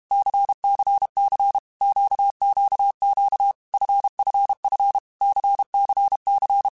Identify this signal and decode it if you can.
Morse code, heard as a keyed tone, about 32 wpm, 800 Hz. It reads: CCC QQQ FFF CCC
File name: Groups@30 / Groups@30 mp3s / LFCYQ@30